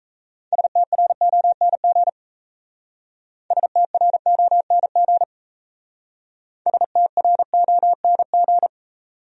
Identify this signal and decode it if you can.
Morse code, heard as a keyed tone, about 35 wpm, 700 Hz. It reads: STRONG STRONG STRONG